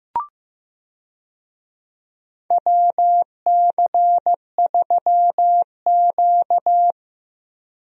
Morse code, heard as a keyed tone, about 15 wpm, 700 Hz. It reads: WC3Q
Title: USA Call Signs - 4 Characters - Repeated 2x - 15wpm